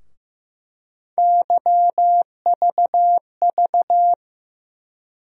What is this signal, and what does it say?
Morse code, heard as a keyed tone, about 15 wpm, 700 Hz. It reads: YVV
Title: USA Call Sign Postfixes - 3 Letters 15wpm